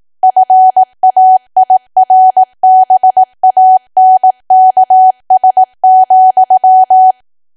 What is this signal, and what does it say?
Morse code, heard as a keyed tone, about 18 wpm, 750 Hz. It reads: FAIRBANKS,
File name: cwa / W1AW-Level2 / 18WPM